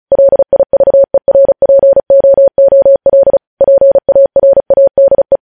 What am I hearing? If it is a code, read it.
Morse code, heard as a keyed tone, about 36 wpm, 550 Hz. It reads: LIVERPOOL PARADE
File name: podcast / News headlines-1x-35x35